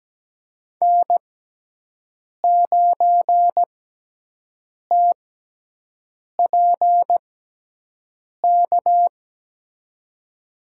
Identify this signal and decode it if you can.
Morse code, heard as a keyed tone, about 17 wpm, spaced as for 6 wpm, 700 Hz. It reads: N9TPK
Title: USA Call Signs - 5 Characters - 6x Character Spacing - 17wpm